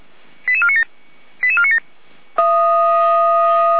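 Five-tone selective call (ZVEI), heard as quick runs of five tones, tones of about 70 ms. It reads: 88497 88497
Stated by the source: ZVEI five-tone selective call received on an SDR.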